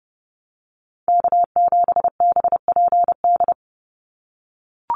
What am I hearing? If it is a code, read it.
Morse code, heard as a keyed tone, about 30 wpm, 700 Hz. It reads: K7BPD
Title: USA Call Signs 30wpm